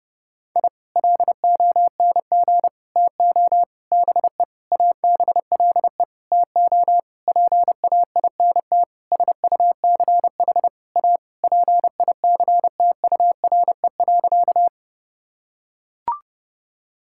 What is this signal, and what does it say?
Morse code, heard as a keyed tone, about 30 wpm, 700 Hz. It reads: I LONG TO BE ABLE TO PAINT SUCH A PICTURE.
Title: Sentences from Top 1000 Words - Review Part 6 - 30wpm